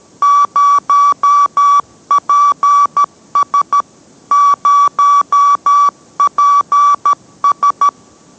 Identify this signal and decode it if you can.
Morse code, heard as a keyed tone, about 13 wpm, 1150 Hz. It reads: 0PS 0PS